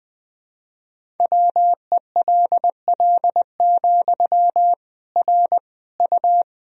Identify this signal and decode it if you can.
Morse code, heard as a keyed tone, about 19 wpm, 700 Hz. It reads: WELL, R U
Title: Sentences from Top 400 Words and Abbreviations 20wpm